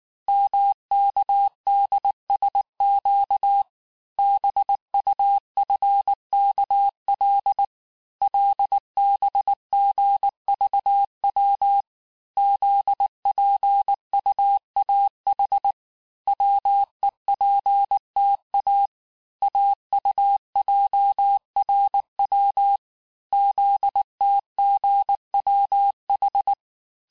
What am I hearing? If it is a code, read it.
Morse code, heard as a keyed tone, about 18 wpm, 800 Hz. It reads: MKDSQ BUFKL LBGVW ZPUAH WEPTA AUJRW ZTGWH
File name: cours-du-mercredi / audios / cours-869-texte-CHIFFRE-vitesse-18-MPM